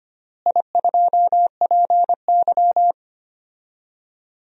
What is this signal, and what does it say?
Morse code, heard as a keyed tone, about 25 wpm, 700 Hz. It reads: I2PY
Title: DXCC-100 Call Signs - 4 Characters - Repeated 3x - 25wpm